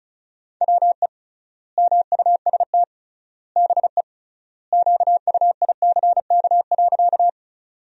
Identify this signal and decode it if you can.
Morse code, heard as a keyed tone, about 34 wpm, 700 Hz. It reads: WE MUST BE QUICK.
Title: 3x Word-Spacing - Sentences from Top 500 Words - 35wpm